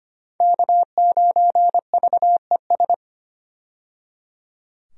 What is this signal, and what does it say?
Morse code, heard as a keyed tone, about 25 wpm, 700 Hz. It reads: K9VES
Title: USA Call Signs - 5 Characters - 25wpm